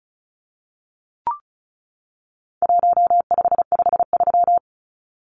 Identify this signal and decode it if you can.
Morse code, heard as a keyed tone, about 34 wpm, 700 Hz. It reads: 1553
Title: Numbers - 4-Digit 35wpm